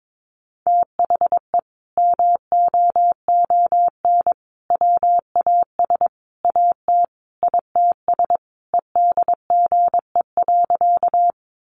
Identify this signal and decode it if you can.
Morse code, heard as a keyed tone, about 22 wpm, 700 Hz. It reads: THE MOON WAS AT ITS EDGE.
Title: Sentences from Top 1300 Words 22wpm